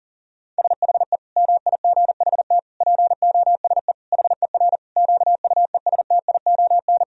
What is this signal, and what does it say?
Morse code, heard as a keyed tone, about 40 wpm, 700 Hz. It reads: SHE MIGHT POSE HER QUESTION
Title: Sentences from Top 1000 Words - Review Part 2 - 40wpm